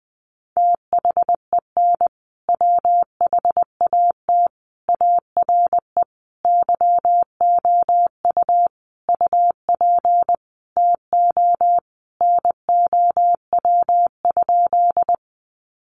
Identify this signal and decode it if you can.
Morse code, heard as a keyed tone, about 20 wpm, 700 Hz. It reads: THEN WHAT ARE YOU UP TO NOW?
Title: Sentences from Top 100 Words 20wpm